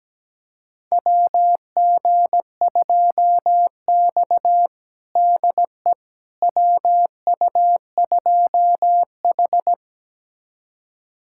Morse code, heard as a keyed tone, about 17 wpm, 700 Hz. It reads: WG2X DE WU2H
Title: USA Callsign DE Callsign - 4 Characters - 17wpm